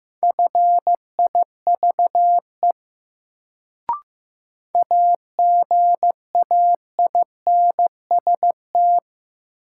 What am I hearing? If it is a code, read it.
Morse code, heard as a keyed tone, about 15 wpm, 700 Hz. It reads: FIVE AGAINST
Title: Sentences from Top 400 Words - New Word Drill - 15wpm